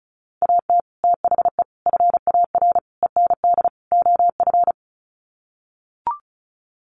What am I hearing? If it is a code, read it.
Morse code, heard as a keyed tone, about 34 wpm, 700 Hz. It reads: AT THE FAR END OF
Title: Sets of 5 Words 35wpm